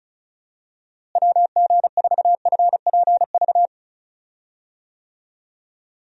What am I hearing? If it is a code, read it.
Morse code, heard as a keyed tone, about 35 wpm, 700 Hz. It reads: WG4FPV
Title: USA Call Signs - 6 Characters - Repeated 2x - 35wpm